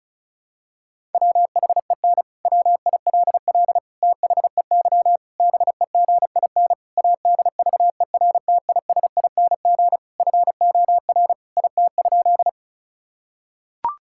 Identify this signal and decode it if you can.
Morse code, heard as a keyed tone, about 34 wpm, 700 Hz. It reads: WHEN WILL THEY BEGIN ADVERTISING FOR IT?
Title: Sentences from Top 1000 Words - Review Part 6 - 35wpm